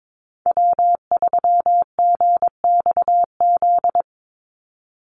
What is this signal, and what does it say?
Morse code, heard as a keyed tone, about 22 wpm, 700 Hz. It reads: W3GXZ
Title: USA Call Signs 22wpm